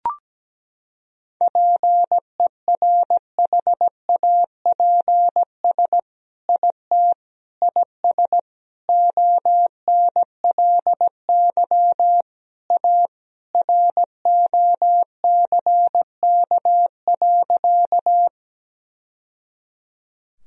Morse code, heard as a keyed tone, about 17 wpm, 700 Hz. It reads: PERHAPS IT IS ONLY A ROCK.
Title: Sentences from Top 1000 Words - Review Part 2 - 17wpm